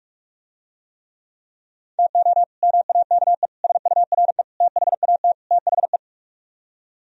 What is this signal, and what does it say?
Morse code, heard as a keyed tone, about 44 wpm, 700 Hz. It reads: TO MAKE SURE THAT THE